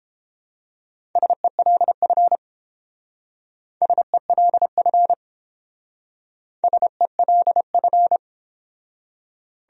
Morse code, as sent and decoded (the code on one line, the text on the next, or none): ... .-.. ..-. / ... .-.. ..-. / ... .-.. ..-.
SELF SELF SELF